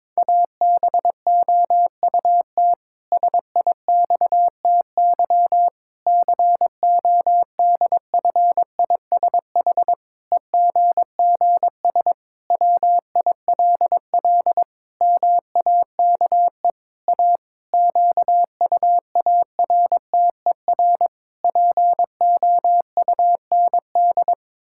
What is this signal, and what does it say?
Morse code, heard as a keyed tone, about 22 wpm, 700 Hz. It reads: ABOUT SIXTY CODFISH EGGS WILL MAKE A QUARTER POUND
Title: Pangram Challenge v2 - 22wpm